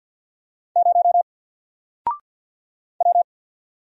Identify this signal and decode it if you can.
Morse code, heard as a keyed tone, about 48 wpm, 700 Hz. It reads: 0 W
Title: Single Character (Letter:Number:Punctuation:Prosign) 50wpm